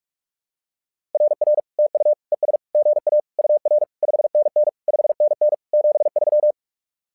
Morse code, heard as a keyed tone, about 47 wpm, 600 Hz. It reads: RR TU ES GA UR 5NN 5NN 73
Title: SOTA QSO Practice 45wpm